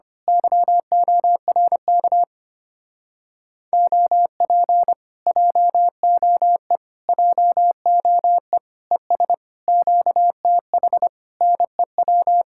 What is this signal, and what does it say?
Morse code, heard as a keyed tone, about 25 wpm, 700 Hz. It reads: YORK OP JOE JOE ES QTH NEW
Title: Speed-Racing Common QSO Phrases Encore - 25wpm